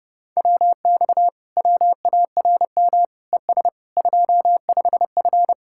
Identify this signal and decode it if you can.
Morse code, heard as a keyed tone, about 30 wpm, 700 Hz. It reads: WX WARM ES 25F